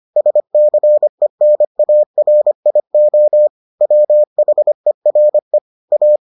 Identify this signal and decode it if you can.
Morse code, heard as a keyed tone, about 25 wpm, 600 Hz. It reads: SCENARIO WHERE A